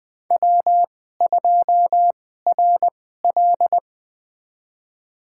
Morse code, heard as a keyed tone, about 20 wpm, spaced as for 14 wpm, 700 Hz. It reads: W2RL
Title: USA Call Signs - 4 Characters - 2x Character Spacing - 20wpm